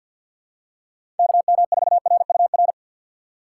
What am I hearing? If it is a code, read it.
Morse code, heard as a keyed tone, about 48 wpm, 700 Hz. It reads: KM4RUR
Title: USA Call Signs - 6 Characters - 6x Character Spacing - 50wpm